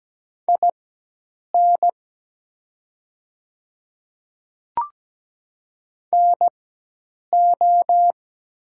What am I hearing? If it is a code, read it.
Morse code, heard as a keyed tone, about 17 wpm, spaced as for 8 wpm, 700 Hz. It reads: IN NO
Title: 4x Character Spacing - Top 100 Words - 17wpm